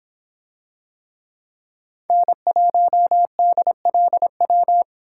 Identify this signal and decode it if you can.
Morse code, heard as a keyed tone, about 26 wpm, 700 Hz. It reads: N1DLW